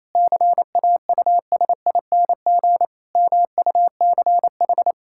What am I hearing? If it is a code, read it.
Morse code, heard as a keyed tone, about 28 wpm, 700 Hz. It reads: CAUSING MUCH
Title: Pangram Challenge v2 - 28wpm